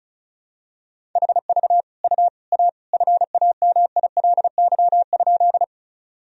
Morse code, heard as a keyed tone, about 34 wpm, 700 Hz. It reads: HV U A FAMILY?
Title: Sentences from Top 400 Words and Abbreviations 35wpm